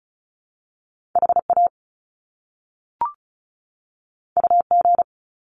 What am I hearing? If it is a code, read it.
Morse code, heard as a keyed tone, about 36 wpm, 700 Hz. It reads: HA UG
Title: Top Two Letter Combinations in Words 35wpm